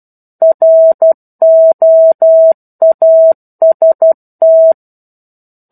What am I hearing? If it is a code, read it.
Morse code, heard as a keyed tone, about 12 wpm, 650 Hz. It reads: ROAST
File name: 01-Chapter-01 12 WPM